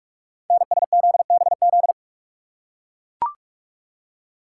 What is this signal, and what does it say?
Morse code, heard as a keyed tone, about 44 wpm, 700 Hz. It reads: NSZBZ
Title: Code Groups - 5 Letters - 45wpm